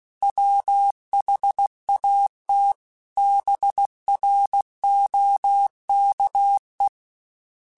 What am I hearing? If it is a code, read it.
Morse code, heard as a keyed tone, about 16 wpm, 800 Hz. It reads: WHAT BROKE
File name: Misc / Misc-mp3s / BIG @15